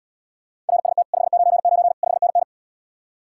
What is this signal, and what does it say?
Morse code, heard as a keyed tone, about 79 wpm, 700 Hz. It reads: RST 599 5NN